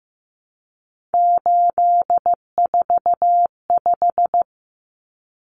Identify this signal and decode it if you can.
Morse code, heard as a keyed tone, about 15 wpm, 700 Hz. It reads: 845
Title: Common Cut Numbers - 3-Digit 15wpm